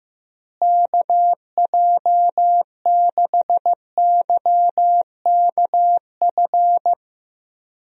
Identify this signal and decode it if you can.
Morse code, heard as a keyed tone, about 15 wpm, 700 Hz. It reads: KJ6YKF
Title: USA Call Signs - 6 Characters - Repeated 2x - 15wpm